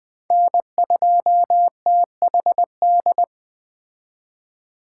Morse code, heard as a keyed tone, about 19 wpm, 700 Hz. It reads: N2THD